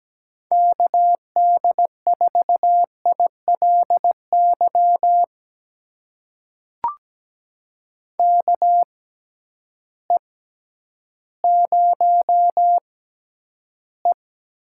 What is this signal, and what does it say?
Morse code, heard as a keyed tone, about 17 wpm, spaced as for 10 wpm, 700 Hz. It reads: KD4ILY KE0E
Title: USA Call Signs - 6 Characters - 6x Character Spacing - 17wpm